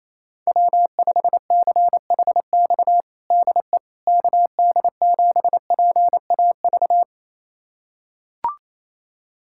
Morse code, heard as a keyed tone, about 28 wpm, 700 Hz. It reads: W5CHX DE KD7PAV
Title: USA Callsign DE Callsign 28wpm